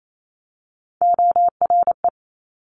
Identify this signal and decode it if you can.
Morse code, heard as a keyed tone, about 28 wpm, 700 Hz. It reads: ORE
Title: Multi-letter Phonemes 28wpm